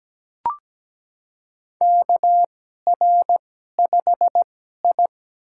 Morse code, heard as a keyed tone, about 17 wpm, spaced as for 12 wpm, 700 Hz. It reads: KR5I
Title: USA Call Signs - 4 Characters - 2x Character Spacing - 17wpm